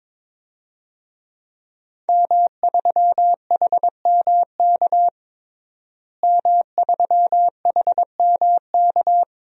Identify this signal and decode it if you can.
Morse code, heard as a keyed tone, about 22 wpm, 700 Hz. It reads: M3HMK M3HMK
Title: DXCC-100 Call Signs - 5 Characters - Repeated 2x - 22wpm